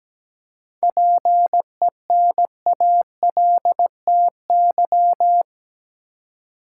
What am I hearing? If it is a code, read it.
Morse code, heard as a keyed tone, about 17 wpm, 700 Hz. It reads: PENALTY
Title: Seven Letter Words - 17wpm